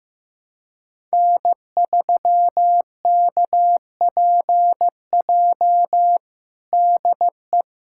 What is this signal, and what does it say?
Morse code, heard as a keyed tone, about 15 wpm, 700 Hz. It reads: N3KPJ DE
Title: USA Callsign DE Callsign 15wpm